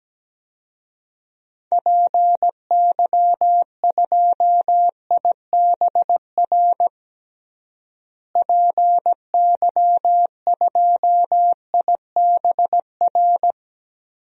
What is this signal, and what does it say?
Morse code, heard as a keyed tone, about 17 wpm, 700 Hz. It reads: PY2IBR PY2IBR
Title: DXCC-100 Call Signs - 6 Characters - Repeated 2x - 17wpm